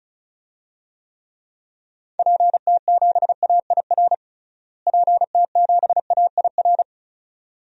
Morse code, heard as a keyed tone, about 35 wpm, 700 Hz. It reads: PT7AIR PT7AIR
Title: DXCC-100 Call Signs - 6 Characters - Repeated 2x - 35wpm